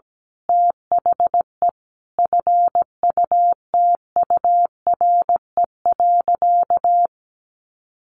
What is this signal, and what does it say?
Morse code, heard as a keyed tone, about 17 wpm, 700 Hz. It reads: THE FUTURE.
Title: A Sound Of Thunder 17wpm